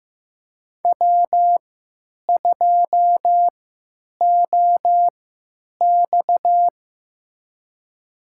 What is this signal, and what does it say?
Morse code, heard as a keyed tone, about 15 wpm, spaced as for 9 wpm, 700 Hz. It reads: W2OX
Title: USA Call Signs - 3x Character Spacing - 15wpm